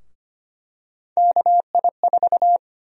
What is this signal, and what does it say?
Morse code, heard as a keyed tone, about 25 wpm, 700 Hz. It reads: KI4
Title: USA Call Sign Prefixes - 25wpm